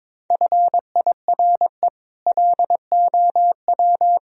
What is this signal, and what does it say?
Morse code, heard as a keyed tone, about 22 wpm, 700 Hz. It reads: FIRE LOW
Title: Sentences from Top 1000 Words - Review Part 1 - 22wpm